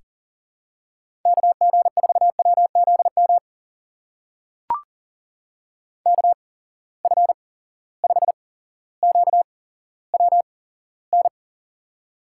Morse code, heard as a keyed tone, about 42 wpm, spaced as for 17 wpm, 700 Hz. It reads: KG4WZM KF5QWN
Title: USA Call Signs - 6 Characters - 8x Character Spacing - 40wpm